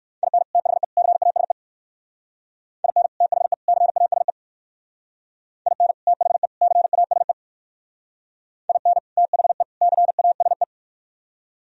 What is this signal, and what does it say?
Morse code, heard as a keyed tone, about 57 wpm, 700 Hz. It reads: IN THE CASE IN THE CASE IN THE CASE IN THE CASE